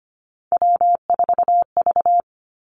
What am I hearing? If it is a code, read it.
Morse code, heard as a keyed tone, about 25 wpm, 700 Hz. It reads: W4V
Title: Three Challenging Letter-Number Combinations 25wpm